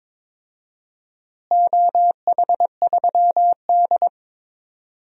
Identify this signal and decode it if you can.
Morse code, heard as a keyed tone, about 22 wpm, 700 Hz. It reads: OH3D
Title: DXCC-100 Call Signs - 4 Characters - Repeated 2x - 22wpm